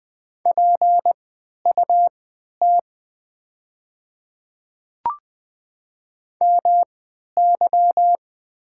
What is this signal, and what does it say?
Morse code, heard as a keyed tone, about 20 wpm, spaced as for 11 wpm, 700 Hz. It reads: PUT MY
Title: 3x Character Spacing - Top 200 Words - 20wpm